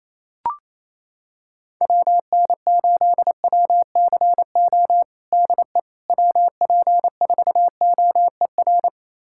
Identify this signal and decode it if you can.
Morse code, heard as a keyed tone, about 28 wpm, 700 Hz. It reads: WN8WCO DE WP4OER
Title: USA Callsign DE Callsign - 6 Characters - 28wpm